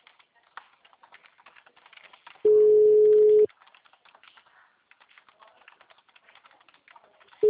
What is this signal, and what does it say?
Ringback tone (Europe, 425 Hz), with speech.
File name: out- -2019- - - .17